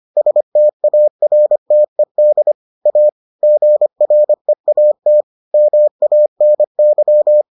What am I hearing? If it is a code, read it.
Morse code, heard as a keyed tone, about 25 wpm, 600 Hz. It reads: STARTED A GREAT MANY